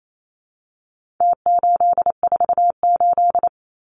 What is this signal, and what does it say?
Morse code, heard as a keyed tone, about 28 wpm, 700 Hz. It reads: T848
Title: Numbers - Contest Style - 4-Digit 28wpm